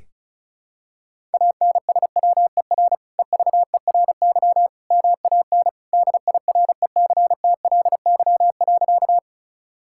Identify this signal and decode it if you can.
Morse code, heard as a keyed tone, about 36 wpm, 700 Hz. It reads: ANSWER EVERY MAN DIRECTLY.
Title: Sentences from Top 1000 Words - Review Part 5 - 35wpm